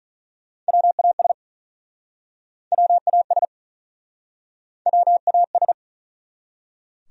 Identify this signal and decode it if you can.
Morse code, heard as a keyed tone, about 44 wpm, 700 Hz. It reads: WAS WAS WAS